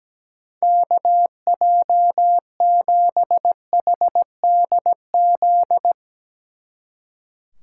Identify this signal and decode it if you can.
Morse code, heard as a keyed tone, about 18 wpm, 700 Hz. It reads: KJ7HDZ
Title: Speed Racing - USA Call Signs 17wpm